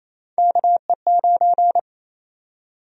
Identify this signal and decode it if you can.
Morse code, heard as a keyed tone, about 27 wpm, 700 Hz. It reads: KE9
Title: USA Call Sign Prefixes - 28wpm